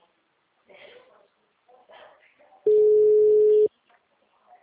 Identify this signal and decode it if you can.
Ringback tone (Europe, 425 Hz), with speech.
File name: out- -2001- - - .54